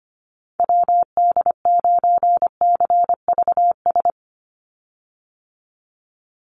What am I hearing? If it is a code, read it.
Morse code, heard as a keyed tone, about 25 wpm, 700 Hz. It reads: WD9CVS
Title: USA Call Signs 25wpm